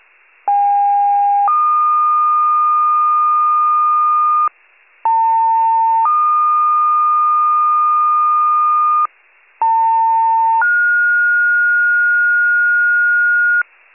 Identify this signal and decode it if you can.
Two-tone page, heard as a one-second tone, then a three-second tone, about 810 then 1220 Hz, about 900 then 1220 Hz, about 900 then 1430 Hz. It is heard from 0.5 to 13.6 s.
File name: wcalert